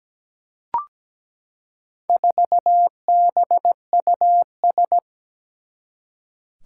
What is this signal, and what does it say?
Morse code, heard as a keyed tone, about 17 wpm, 700 Hz. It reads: 4BUS
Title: Four Challenging Letter-Number Combinations 17wpm